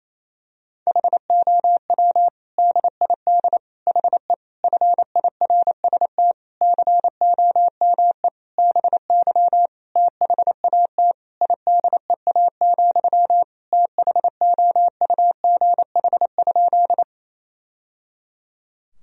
Morse code, heard as a keyed tone, about 28 wpm, 700 Hz. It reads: HOW DID HE FIRST COME BY THAT IDEA, THOUGH?
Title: Sentences from Top 1000 Words - Review Part 2 - 28wpm